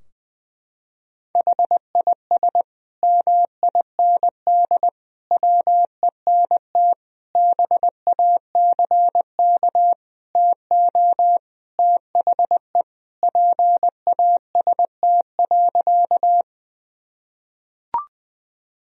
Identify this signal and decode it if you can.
Morse code, heard as a keyed tone, about 21 wpm, 700 Hz. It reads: HIS MIND WENT BACK TO THE PAST.
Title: Sentences from Top 1000 Words - Review Part 3 - 20wpm